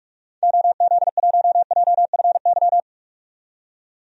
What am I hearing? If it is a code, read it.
Morse code, heard as a keyed tone, about 44 wpm, 700 Hz. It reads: OZ1JFY
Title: DXCC-100 Call Signs - 6 Characters - 45wpm